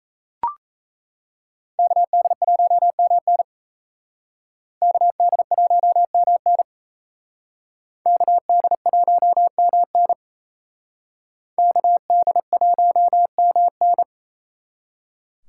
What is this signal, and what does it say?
Morse code, heard as a keyed tone, about 36 wpm, 700 Hz. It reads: KD1MN KD1MN KD1MN KD1MN